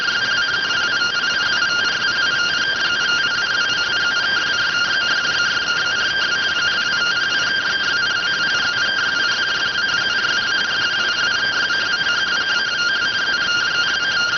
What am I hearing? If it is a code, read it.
Two-tone teleprinter signal, about 170 Hz shift.